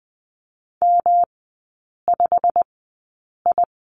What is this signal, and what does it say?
Morse code, heard as a keyed tone, about 20 wpm, spaced as for 8 wpm, 700 Hz. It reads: M5I
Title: 172 Lesson 28e - Number 9 - Three Characters 20wpm